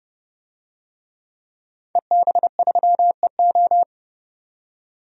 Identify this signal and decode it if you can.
Morse code, heard as a keyed tone, about 28 wpm, 690 Hz. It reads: EB3EO
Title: DXCC-100 Call Signs - 5 Characters - 30wpm